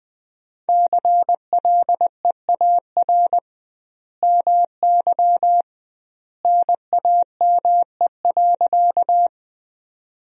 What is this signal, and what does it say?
Morse code, heard as a keyed tone, about 20 wpm, 700 Hz. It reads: CLEAR MY NAME.
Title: 2x Word-Spacing - Sentences from Top 500 Words - 20wpm